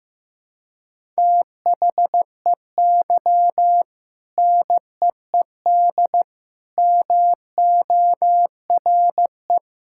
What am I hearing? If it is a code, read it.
Morse code, heard as a keyed tone, about 15 wpm, 700 Hz. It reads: THEY NEED MORE
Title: Sentences from Top 1000 Words - Review Part 6 - 15wpm